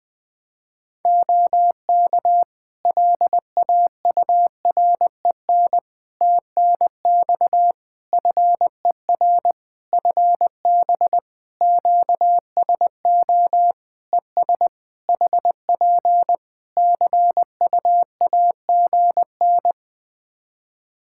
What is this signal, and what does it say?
Morse code, heard as a keyed tone, about 20 wpm, 700 Hz. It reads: OK LAUREN TNX FER FB QSO ES HP CUAGN
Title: Practice QSO - Exchange 4 - Ending - 20wpm